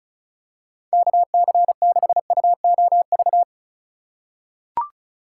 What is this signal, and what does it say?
Morse code, heard as a keyed tone, about 34 wpm, 700 Hz. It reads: KC6UOV
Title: USA Call Signs - 6 Characters - 2x Character Spacing - 35wpm